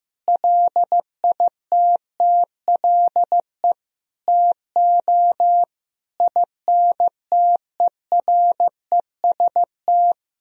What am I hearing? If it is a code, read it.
Morse code, heard as a keyed tone, about 15 wpm, 700 Hz. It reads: LITTLE TO INTEREST
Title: Sentences from Top 1000 Words - Review Part 2 - 15wpm